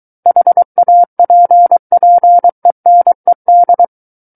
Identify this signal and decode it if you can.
Morse code, heard as a keyed tone, about 23 wpm, 700 Hz. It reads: HAPPENED
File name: Book One / 14-Chapter 14 - 23 WPM